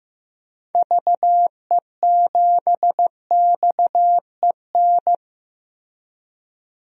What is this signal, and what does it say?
Morse code, heard as a keyed tone, about 15 wpm, 700 Hz. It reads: VE7XEN
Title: DXCC-100 Call Signs - 6 Characters - Repeated 2x - 15wpm